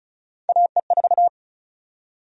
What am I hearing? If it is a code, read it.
Morse code, heard as a keyed tone, about 36 wpm, 700 Hz. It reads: AE4
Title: USA Call Sign Prefixes - Repeated 2x - 35wpm